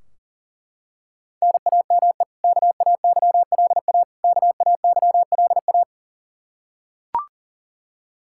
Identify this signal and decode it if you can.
Morse code, heard as a keyed tone, about 37 wpm, 700 Hz. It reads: NAME KAYLA KAYLA
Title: Practice QSO - Exchange 1 - Beginning - 40wpm